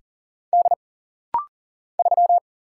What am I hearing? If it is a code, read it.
Morse code, heard as a keyed tone, about 40 wpm, 700 Hz. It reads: D 3
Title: Single Letter-Number Alt - Rapid-Fire 40wpm